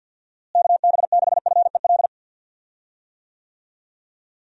Morse code, heard as a keyed tone, about 50 wpm, 700 Hz. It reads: KB6FEL
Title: USA Call Signs - 6 Characters - Repeated 2x - 50wpm